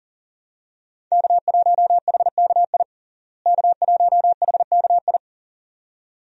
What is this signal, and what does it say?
Morse code, heard as a keyed tone, about 40 wpm, 700 Hz. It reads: K1HKI K1HKI
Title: USA Call Signs - 5 Characters - Repeated 2x - 40wpm